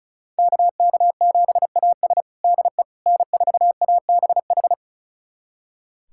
Morse code, heard as a keyed tone, about 36 wpm, 700 Hz. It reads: KK7AS DE N4ABH
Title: USA Callsign DE Callsign - 5 Characters - 35wpm